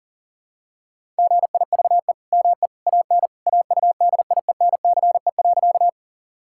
Morse code, heard as a keyed tone, about 40 wpm, 700 Hz. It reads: GIVE ME AN AUDIENCE.
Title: Sentences from Top 1000 Words - Review Part 5 - 40wpm